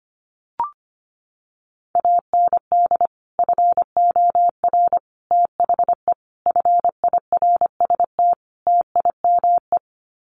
Morse code, heard as a keyed tone, about 25 wpm, 700 Hz. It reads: AND FOR THE FIRST TIME
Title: Sets of 5 Words 25wpm